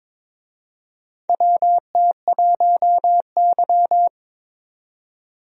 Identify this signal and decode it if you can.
Morse code, heard as a keyed tone, about 22 wpm, 700 Hz. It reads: WT1Y
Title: USA Call Signs - 4 Characters - 22wpm